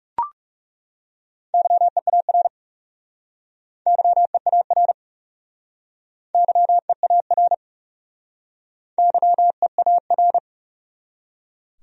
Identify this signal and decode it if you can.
Morse code, heard as a keyed tone, about 39 wpm, 700 Hz. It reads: YEAR YEAR YEAR YEAR